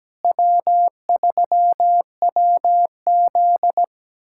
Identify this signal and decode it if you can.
Morse code, heard as a keyed tone, about 17 wpm, 700 Hz. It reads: W3WZ
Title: USA Callsign DE Callsign 17wpm